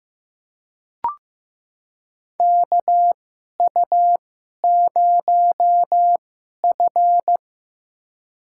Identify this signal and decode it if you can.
Morse code, heard as a keyed tone, about 15 wpm, spaced as for 11 wpm, 700 Hz. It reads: KU0F
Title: USA Call Signs - 4 Characters - 2x Character Spacing - 15wpm